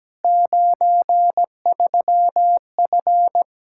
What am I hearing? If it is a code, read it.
Morse code, heard as a keyed tone, about 17 wpm, 700 Hz. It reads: 93F
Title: K1USN SST QSO Practice 17wpm